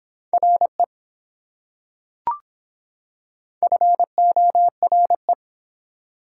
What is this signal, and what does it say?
Morse code, heard as a keyed tone, about 24 wpm, 700 Hz. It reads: RE FORE